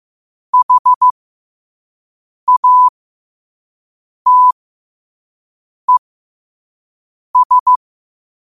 Morse code, heard as a keyed tone, about 15 wpm, spaced as for 5 wpm, 1000 Hz. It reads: HATES
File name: qotdfiles / 5wpm